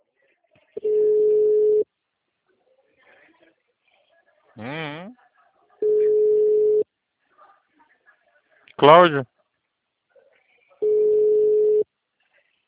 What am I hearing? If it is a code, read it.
Ringback tone (Europe, 425 Hz), with speech.